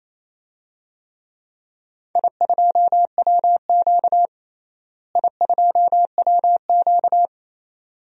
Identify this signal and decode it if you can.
Morse code, heard as a keyed tone, about 28 wpm, 700 Hz. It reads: I2WQ I2WQ
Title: DXCC-100 Call Signs - 4 Characters - Repeated 2x - 28wpm